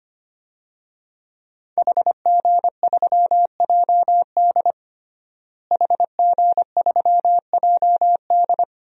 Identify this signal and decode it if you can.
Morse code, heard as a keyed tone, about 25 wpm, 700 Hz. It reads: HG3JD HG3JD
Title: DXCC-100 Call Signs - 5 Characters - Repeated 3x - 25wpm